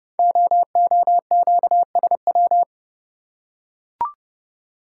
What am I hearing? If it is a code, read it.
Morse code, heard as a keyed tone, about 30 wpm, 690 Hz. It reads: OOQSW
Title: Code Groups - 5 Letters - 30wpm